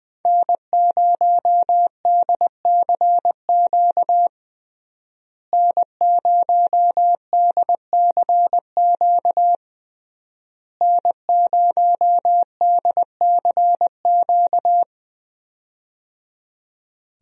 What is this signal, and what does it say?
Morse code, heard as a keyed tone, about 20 wpm, 700 Hz. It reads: N0DCQ N0DCQ N0DCQ
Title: USA Call Signs - 5 Characters - Repeated 3x - 20wpm